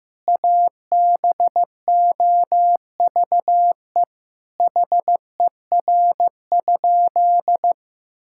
Morse code, heard as a keyed tone, about 15 wpm, 700 Hz. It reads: ABOVE HER?
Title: Sentences from Top 1000 Words - Review Part 5 - 15wpm